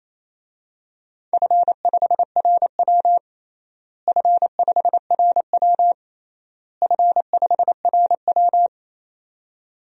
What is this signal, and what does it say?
Morse code, heard as a keyed tone, about 28 wpm, 700 Hz. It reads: F5RW F5RW F5RW
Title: DXCC-100 Call Signs - 4 Characters - Repeated 3x - 28wpm